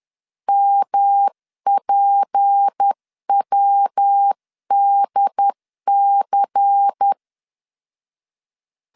Morse code, heard as a keyed tone, about 10 wpm, 800 Hz. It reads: MPWDC
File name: CW / DET / 1 6 1500C S C 10wpm 4252